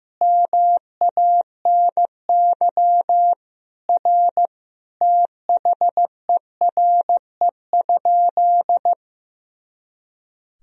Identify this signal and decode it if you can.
Morse code, heard as a keyed tone, about 15 wpm, 700 Hz. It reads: MANY R THERE?
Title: Sentences from Top 100 Words and Abbreviations 15wpm